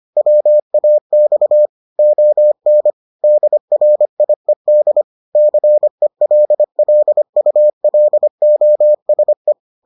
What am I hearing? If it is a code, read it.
Morse code, heard as a keyed tone, about 25 wpm, 600 Hz. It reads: WAX ON DRIED CELLULOSE